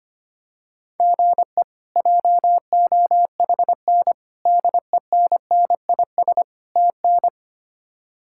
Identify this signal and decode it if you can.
Morse code, heard as a keyed tone, about 25 wpm, 700 Hz. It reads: GE JOHN DENNIS TN
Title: K1USN SST QSO Practice - 25wpm